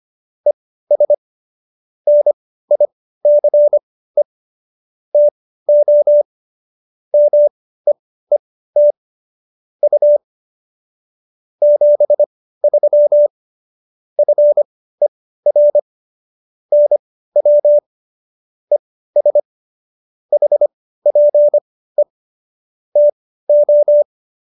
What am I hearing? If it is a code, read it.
Morse code, heard as a keyed tone, about 25 wpm, spaced as for 15 wpm, 590 Hz. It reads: ES NICE TO MEET U 73 FER NW ES HPE TO